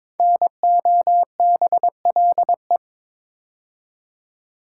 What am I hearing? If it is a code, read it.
Morse code, heard as a keyed tone, about 22 wpm, 700 Hz. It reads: NOBLE
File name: 5 Letter Words - First 3 Letters Given - 22wpm